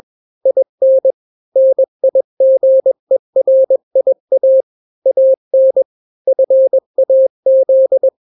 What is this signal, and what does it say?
Morse code, heard as a keyed tone, about 21 wpm, 530 Hz. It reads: IN NIGERIA AN FAZ
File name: daily / 20wpm